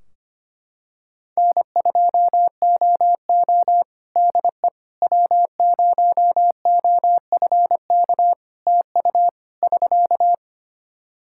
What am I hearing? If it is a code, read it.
Morse code, heard as a keyed tone, about 25 wpm, 700 Hz. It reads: N2OO DE W0OFK TU <SK>
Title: K1USN SST QSO Practice 25wpm